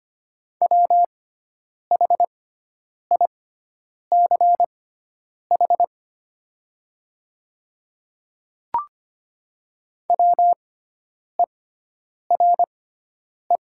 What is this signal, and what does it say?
Morse code, heard as a keyed tone, about 25 wpm, spaced as for 9 wpm, 700 Hz. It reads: WHICH WERE
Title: 6x Character Spacing - Top 200 Words - 25wpm